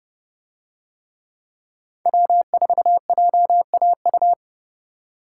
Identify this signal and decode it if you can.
Morse code, heard as a keyed tone, about 30 wpm, 700 Hz. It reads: W4JAU